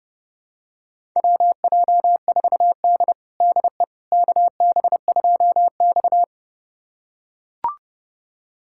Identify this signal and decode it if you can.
Morse code, heard as a keyed tone, about 30 wpm, 700 Hz. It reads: WJ4D DE KB2X
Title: USA Callsign DE Callsign - 4 Characters - 30wpm